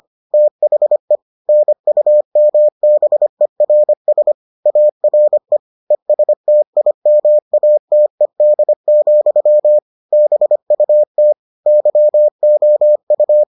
Morse code, heard as a keyed tone, about 25 wpm, 600 Hz. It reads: THE NUMBERS ARE ESTIMATED, BUT YOU